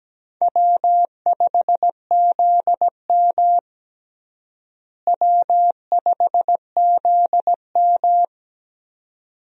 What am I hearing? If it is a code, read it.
Morse code, heard as a keyed tone, about 17 wpm, 700 Hz. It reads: W5ZM W5ZM
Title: USA Call Signs - 4 Characters - Repeated 2x - 17wpm